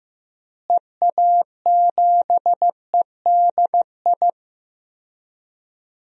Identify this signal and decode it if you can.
Morse code, heard as a keyed tone, about 15 wpm, 700 Hz. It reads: EA7EDI
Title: DXCC-100 Call Signs - 6 Characters - Repeated 2x - 15wpm